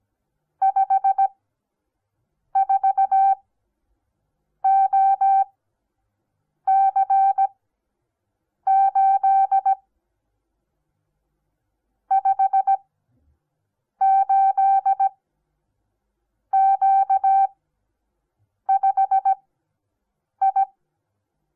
Morse code, heard as a keyed tone, about 17 wpm, spaced as for 6 wpm, 780 Hz. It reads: 54OC8 58Q5I
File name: Onlinelehrgang / morsen / DJ4UF Morsen08